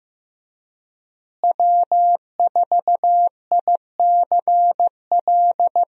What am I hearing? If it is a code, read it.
Morse code, heard as a keyed tone, about 16 wpm, 700 Hz. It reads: W4ICL
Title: USA Call Signs - 5 Characters - Repeated 3x - 15wpm